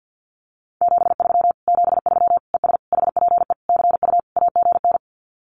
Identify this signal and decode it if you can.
Morse code, heard as a keyed tone, about 50 wpm, 700 Hz. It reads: 73 73 ES HPE CU AGN
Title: Common QSO Phrases Encore 50wpm